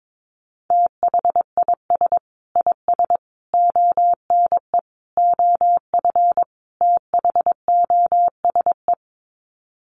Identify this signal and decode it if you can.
Morse code, heard as a keyed tone, about 22 wpm, 700 Hz. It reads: THIS IS ONE OF THOSE
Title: Sets of 5 Words Encore2 22wpm